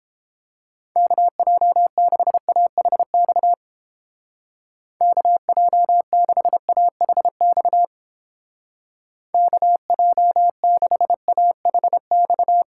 Morse code, heard as a keyed tone, about 29 wpm, 700 Hz. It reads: KJ6AHX KJ6AHX KJ6AHX